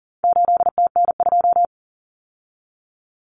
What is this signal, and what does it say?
Morse code, heard as a keyed tone, about 40 wpm, 700 Hz. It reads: 8TN2
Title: Common Cut Numbers - 4-Digit 40wpm